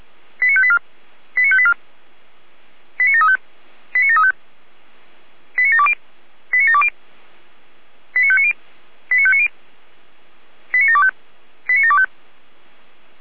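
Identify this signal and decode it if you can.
Five-tone selective call (ZVEI), heard as quick runs of five tones, tones of about 80 ms. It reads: 89584 89584 89735 89735 89720 89720 89590 89590 89725 89725
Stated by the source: ZVEI five-tone selective call 89706 received on SDR.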